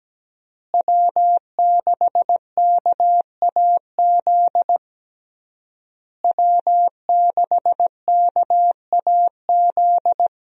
Morse code, heard as a keyed tone, about 17 wpm, 700 Hz. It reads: W6KAZ W6KAZ
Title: USA Call Signs - 5 Characters - Repeated 3x - 17wpm